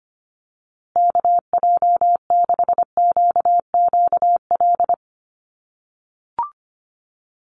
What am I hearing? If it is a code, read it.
Morse code, heard as a keyed tone, about 25 wpm, 700 Hz. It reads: KJ6QQL
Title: USA Call Signs Repeated 3x 25wpm